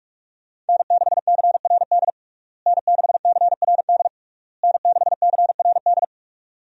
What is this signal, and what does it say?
Morse code, heard as a keyed tone, about 41 wpm, 700 Hz. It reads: N6CRD N6CRD N6CRD
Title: USA Call Signs - 5 Characters - Repeated 3x - 45wpm